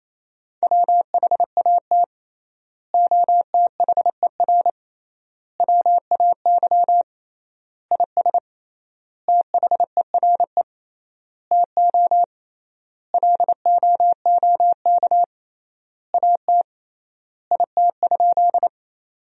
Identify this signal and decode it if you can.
Morse code, heard as a keyed tone, about 28 wpm, 700 Hz. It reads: WHAT OTHER WAY IS THERE TO LOOK AT IT?
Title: 3x Word-Spacing - Sentences from Top 100 Words - 28wpm